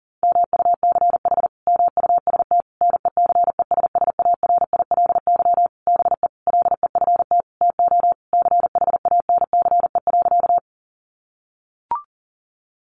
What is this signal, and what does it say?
Morse code, heard as a keyed tone, about 40 wpm, 700 Hz. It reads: MUCH MUST NECESSARILY BE LEFT TO CHANCE.
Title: Sentences from Top 1400 Words 40wpm